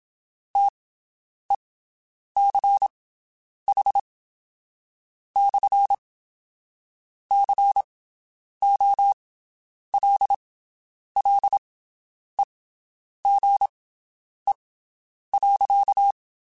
Morse code, heard as a keyed tone, about 27 wpm, spaced as for 8 wpm, 800 Hz. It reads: TECH/COLLEGE.
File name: (spaced) / QSOs Spaced-mp3s / WD0BGP 1 @25 sp